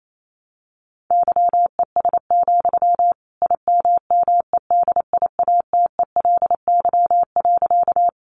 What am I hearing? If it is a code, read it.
Morse code, heard as a keyed tone, about 28 wpm, 700 Hz. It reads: YES, IMMEDIATELY.
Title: Sentences from Top 1100 Words 28wpm